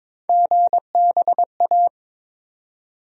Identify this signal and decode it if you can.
Morse code, heard as a keyed tone, about 22 wpm, 700 Hz. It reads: GBA
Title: USA Call Sign Postfixes - 3 Letters 22wpm